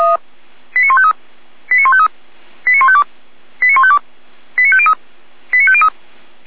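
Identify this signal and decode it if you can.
Five-tone selective call (ZVEI), heard as quick runs of five tones, tones of about 70 ms. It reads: 89153 89153 89152 89152 89503 89503
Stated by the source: ZVEI five-tone selective call received on SDR.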